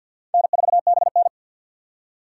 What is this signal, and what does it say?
Morse code, heard as a keyed tone, about 51 wpm, 700 Hz. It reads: N4BN
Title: USA Call Signs - 4 Characters - 3x Character Spacing - 50wpm